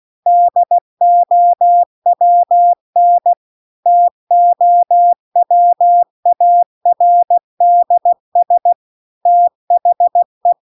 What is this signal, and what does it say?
Morse code, heard as a keyed tone, about 16 wpm, 700 Hz. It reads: DOWN TOWARDS THE